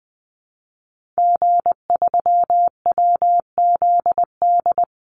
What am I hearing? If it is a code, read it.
Morse code, heard as a keyed tone, about 20 wpm, 700 Hz. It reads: G3WZD
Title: Call Signs Repeated 2x - 20wpm